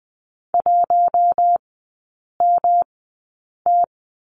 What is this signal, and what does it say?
Morse code, heard as a keyed tone, about 20 wpm, spaced as for 8 wpm, 700 Hz. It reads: 1MT
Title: 114 Lesson 19e - Letter w - Three Characters 20wpm